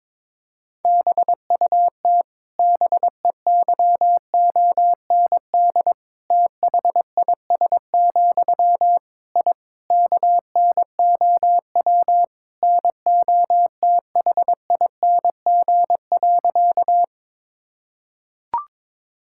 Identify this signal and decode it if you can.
Morse code, heard as a keyed tone, about 22 wpm, 700 Hz. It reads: BUT BEYOND THIS, I KNOW NOTHING.
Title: Sentences from Top 1000 Words - Review Part 6 - 22wpm